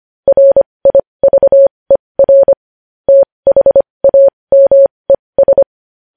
Morse code, heard as a keyed tone, about 25 wpm, spaced as for 20 wpm, 550 Hz. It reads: RIVER THAMES
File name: podcast / News headlines-1x-25x20